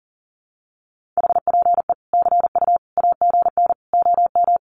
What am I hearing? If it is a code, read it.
Morse code, heard as a keyed tone, about 40 wpm, 700 Hz. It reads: HPE CU AGN OM
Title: Common QSO Phrases 40wpm